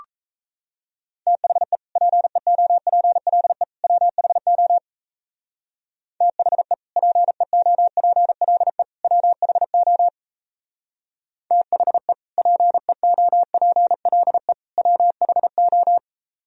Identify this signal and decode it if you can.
Morse code, heard as a keyed tone, about 34 wpm, 700 Hz. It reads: THE PEOPLE WHO THE PEOPLE WHO THE PEOPLE WHO
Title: Speed-Racing - Sets of 3 Words 28wpm